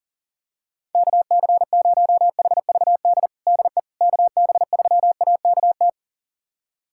Morse code, heard as a keyed tone, about 40 wpm, 700 Hz. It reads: KC0HVD DE KB3AKT
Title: USA Callsign DE Callsign 40wpm